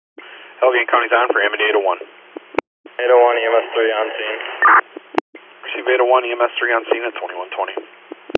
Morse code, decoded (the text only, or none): none